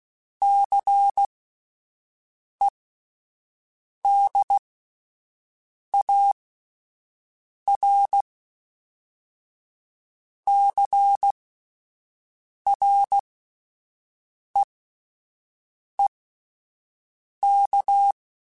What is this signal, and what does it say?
Morse code, heard as a keyed tone, about 16 wpm, spaced as for 5 wpm, 800 Hz. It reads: CEDARCREEK